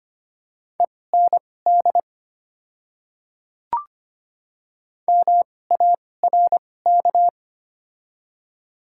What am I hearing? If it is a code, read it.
Morse code, heard as a keyed tone, about 25 wpm, spaced as for 18 wpm, 690 Hz. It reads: END MARK
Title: 2x Character Spacing - Top 300 Words - 25wpm